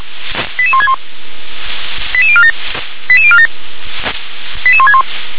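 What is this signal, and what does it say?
Five-tone selective call (ZVEI), heard as quick runs of five tones, tones of about 70 ms. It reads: 88161 88047 88047 88161
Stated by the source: ZVEI five-tone selective call 88182, received on an SDR.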